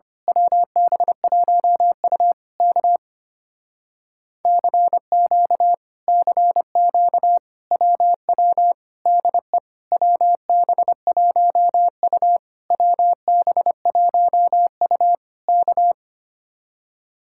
Morse code, heard as a keyed tone, about 24 wpm, 700 Hz. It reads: WB1U K CQ CQ WW DE WB1U WB1U K
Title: Speed-Racing Common QSO Phrases Encore - 25wpm